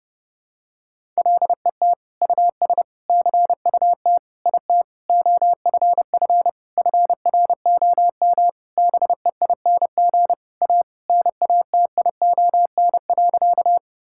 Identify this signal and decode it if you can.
Morse code, heard as a keyed tone, about 30 wpm, 700 Hz. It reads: LET US CUT IT OFF FROM BEING A NATION.
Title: Sentences from Top 1000 Words - Review Part 6 - 30wpm